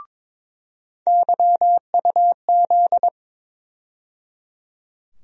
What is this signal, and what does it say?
Morse code, heard as a keyed tone, about 22 wpm, 700 Hz. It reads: YUZ
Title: USA Call Sign Postfixes - 3 Letters 22wpm